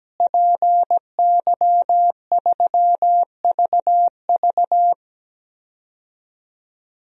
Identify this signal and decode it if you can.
Morse code, heard as a keyed tone, about 17 wpm, 700 Hz. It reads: PY3VV
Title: DXCC-100 Call Signs - 5 Characters - Repeated 2x - 17wpm